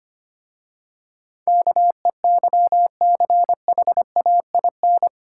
Morse code, heard as a keyed tone, about 25 wpm, 700 Hz. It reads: KEYCHAIN
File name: practice / 25wpm / Difficult Words with Hints - 25wpm